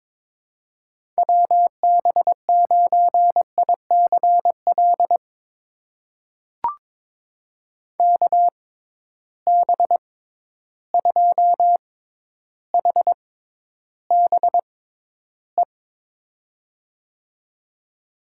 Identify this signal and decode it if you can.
Morse code, heard as a keyed tone, about 22 wpm, spaced as for 11 wpm, 700 Hz. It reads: WB9ICL KB2HBE
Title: USA Call Signs - 6 Characters - 6x Character Spacing - 22wpm